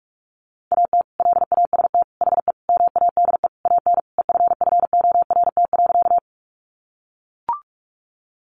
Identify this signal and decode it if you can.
Morse code, heard as a keyed tone, about 44 wpm, 700 Hz. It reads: AT LAST HE MADE AN EFFORT.
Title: Sentences from Top 1200 Words 45wpm